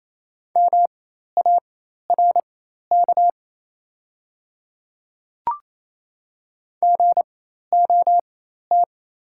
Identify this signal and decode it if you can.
Morse code, heard as a keyed tone, about 27 wpm, spaced as for 13 wpm, 700 Hz. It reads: MARK GOT
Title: 4x Character Spacing - Top 300 Words - 28wpm